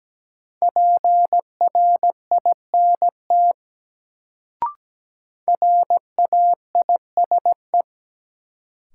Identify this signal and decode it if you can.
Morse code, heard as a keyed tone, about 17 wpm, 700 Hz. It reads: PRINT RAISE
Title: Sentences from Top 1400 Words - New Word Drill Alt - 17wpm